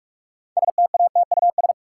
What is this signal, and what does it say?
Morse code, heard as a keyed tone, about 46 wpm, 700 Hz. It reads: STATUS
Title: Sentences from Top 1400 Words - New Word Drill Alt - 45wpm